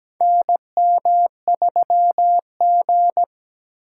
Morse code, heard as a keyed tone, about 17 wpm, 700 Hz. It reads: NM3G
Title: USA Call Signs - 4 Characters - 4x Character Spacing - 17wpm